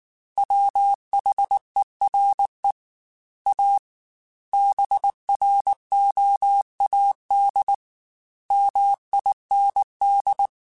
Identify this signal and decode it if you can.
Morse code, heard as a keyed tone, about 19 wpm, 800 Hz. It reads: WHERE A BROAD MIND
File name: Chuckles / Chuckles-mp3s / MiddleAge @18